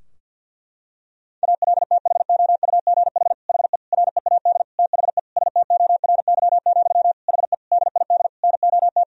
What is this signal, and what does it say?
Morse code, heard as a keyed tone, about 51 wpm, 700 Hz. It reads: ALTHOUGH HE READ THE STORY, HE DID NOT
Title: Sentences from Top 1000 Words - Review Part 3 - 50wpm